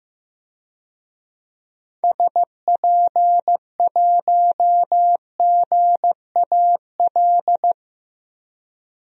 Morse code, heard as a keyed tone, about 15 wpm, 700 Hz. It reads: SP1GAL
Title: DXCC-100 Call Signs - 6 Characters - 15wpm